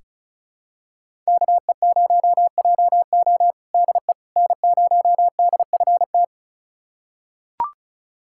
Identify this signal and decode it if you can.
Morse code, heard as a keyed tone, about 36 wpm, 700 Hz. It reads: KE0JO DE N0DFT
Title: USA Callsign DE Callsign - 5 Characters - 35wpm